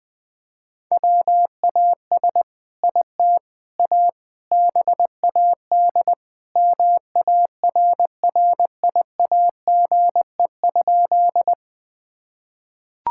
Morse code, heard as a keyed tone, about 21 wpm, 700 Hz. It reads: WAS IT A BAD MARRIAGE?
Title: Sentences from Top 1000 Words - Review Part 5 - 20wpm